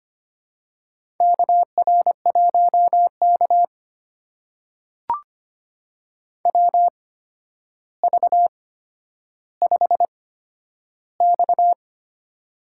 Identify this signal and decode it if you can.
Morse code, heard as a keyed tone, about 25 wpm, spaced as for 11 wpm, 700 Hz. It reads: KR1K WV5X
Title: USA Call Signs - 4 Characters - 8x Character Spacing - 25wpm